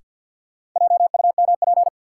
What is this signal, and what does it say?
Morse code, heard as a keyed tone, about 46 wpm, 700 Hz. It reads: JUMP
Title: Sentences from Top 1300 Words - New Word Drill Alt - 50wpm